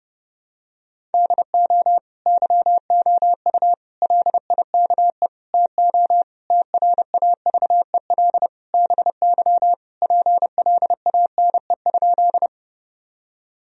Morse code, heard as a keyed tone, about 30 wpm, 700 Hz. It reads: DO YOU LIKE TO TRAVEL BY PLANE?
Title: Sentences from Top 1000 Words - Review Part 3 - 30wpm